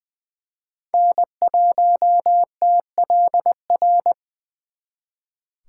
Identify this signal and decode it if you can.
Morse code, heard as a keyed tone, about 24 wpm, 700 Hz. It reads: N1TLR
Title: Speed Racing - USA 1x3 Call Signs 20wpm